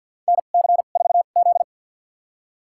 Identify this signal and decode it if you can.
Morse code, heard as a keyed tone, about 53 wpm, spaced as for 37 wpm, 700 Hz. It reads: NC4Z
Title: USA Call Signs - 4 Characters - 2x Character Spacing - 50wpm